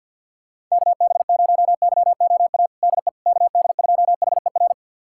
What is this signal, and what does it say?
Morse code, heard as a keyed tone, about 51 wpm, 700 Hz. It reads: KB0YOA DE KD2HER
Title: USA Callsign DE Callsign - 6 Characters - 50wpm